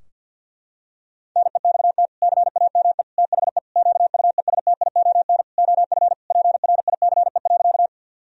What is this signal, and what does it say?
Morse code, heard as a keyed tone, about 52 wpm, 700 Hz. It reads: NEXT CAME THE QUESTION OF PRICE.
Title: Sentences from Top 1000 Words - Review Part 4 - 50wpm